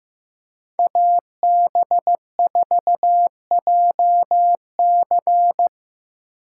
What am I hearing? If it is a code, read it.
Morse code, heard as a keyed tone, about 15 wpm, 700 Hz. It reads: AB4JC
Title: USA Call Signs - 2x Character Spacing - 15wpm